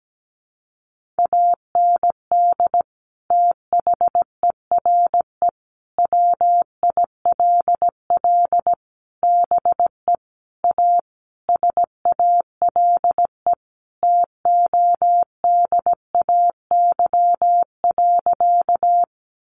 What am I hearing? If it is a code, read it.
Morse code, heard as a keyed tone, about 17 wpm, 700 Hz. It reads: AND THERE WILL BE A SALE TODAY.
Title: Sentences from Top 1400 Words 17wpm